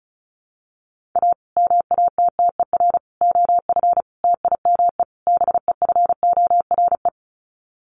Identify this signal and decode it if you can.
Morse code, heard as a keyed tone, about 34 wpm, 700 Hz. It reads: A MATTER OF TIME BEFORE
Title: Sets of 5 Words Encore 35wpm